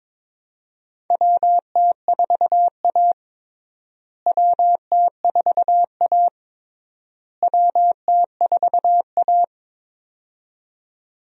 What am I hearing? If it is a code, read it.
Morse code, heard as a keyed tone, about 22 wpm, 700 Hz. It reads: WT4A WT4A WT4A
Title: USA Call Signs - 4 Characters - Repeated 3x - 22wpm